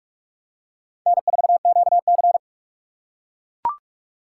Morse code, heard as a keyed tone, about 44 wpm, 700 Hz. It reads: N4QC